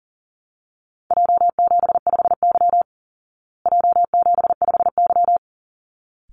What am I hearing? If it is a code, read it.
Morse code, heard as a keyed tone, about 40 wpm, 700 Hz. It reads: J75Y J75Y
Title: Call Signs Repeated 3x - 40wpm